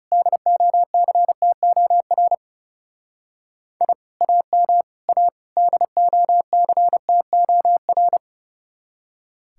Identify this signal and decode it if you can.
Morse code, heard as a keyed tone, about 36 wpm, 700 Hz. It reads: DOCTOR I AM A DOCTOR
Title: Speed-Racing Common QSO Phrases Encore - 30wpm